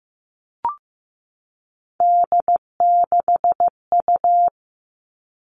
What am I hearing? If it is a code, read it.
Morse code, heard as a keyed tone, about 15 wpm, 700 Hz. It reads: D6U
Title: All Cut Numbers - 3-Digit 15wpm